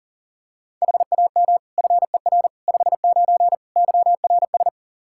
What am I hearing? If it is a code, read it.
Morse code, heard as a keyed tone, about 40 wpm, 700 Hz. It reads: HAM FER 59 YRS
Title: Practice QSO - Exchange 3 - More Info - 40wpm